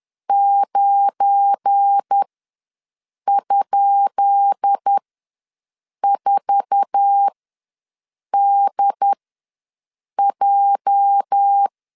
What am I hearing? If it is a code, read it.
Morse code, heard as a keyed tone, about 10 wpm, spaced as for 6 wpm, 800 Hz. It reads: 9?4DJ
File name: LEZIONI / 10 SIGNS / 10B INT ws 10wpm 4134